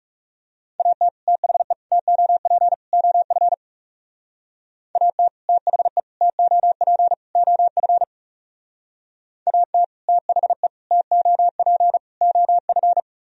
Morse code, heard as a keyed tone, about 38 wpm, 700 Hz. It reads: AT THE TOP OF AT THE TOP OF AT THE TOP OF
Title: Speed-Racing - Sets of 4 Words 30wpm